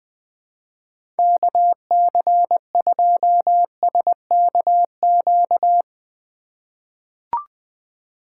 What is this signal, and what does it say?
Morse code, heard as a keyed tone, about 21 wpm, 700 Hz. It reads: KC2SKQ
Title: USA Call Signs - 6 Characters - 4x Character Spacing - 20wpm